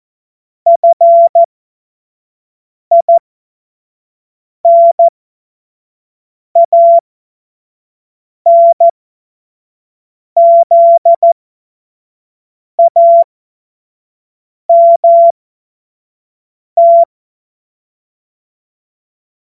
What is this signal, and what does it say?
Morse code, heard as a keyed tone, about 14 wpm, spaced as for 5 wpm, 680 Hz. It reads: FINANZAMT
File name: Morsekurs / Witze / Witz 16. Mai 2023 25 F70